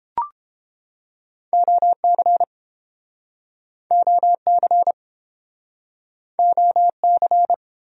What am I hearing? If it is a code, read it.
Morse code, heard as a keyed tone, about 30 wpm, 700 Hz. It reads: OC OC OC